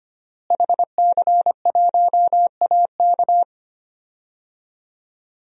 Morse code, heard as a keyed tone, about 25 wpm, 700 Hz. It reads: HC1AK
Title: DXCC-100 Call Signs - 5 Characters - Repeated 2x - 25wpm